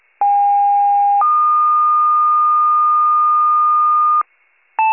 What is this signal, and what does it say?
Two-tone page, heard as a one-second tone, then a three-second tone, about 810 then 1220 Hz. It is heard from 0.2 to 4.2 s.